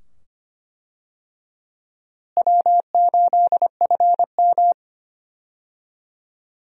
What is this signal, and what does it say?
Morse code, heard as a keyed tone, about 25 wpm, 700 Hz. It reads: W8FM
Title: USA Call Signs - 4 Characters - Repeated 2x - 25wpm